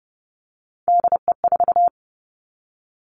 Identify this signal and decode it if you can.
Morse code, heard as a keyed tone, about 30 wpm, 700 Hz. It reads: DE4
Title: Three Challenging Letter-Number Combinations 30wpm